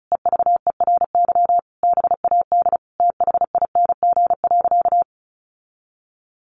Morse code, heard as a keyed tone, about 34 wpm, 700 Hz. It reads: EVERY BAD THING.
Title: Sentences from Top 1200 Words 35wpm